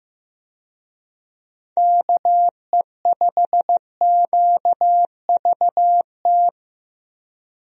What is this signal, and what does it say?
Morse code, heard as a keyed tone, about 15 wpm, 700 Hz. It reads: KE5QVT
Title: USA Call Signs - 6 Characters - 15wpm